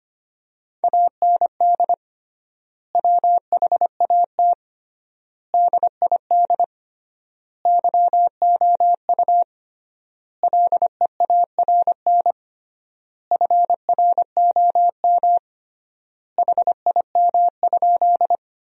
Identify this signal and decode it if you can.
Morse code, heard as a keyed tone, about 25 wpm, 700 Hz. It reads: AND WHAT DID YOU LEARN FROM HIM?
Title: 3x Word-Spacing - Sentences from Top 300 Words - 25wpm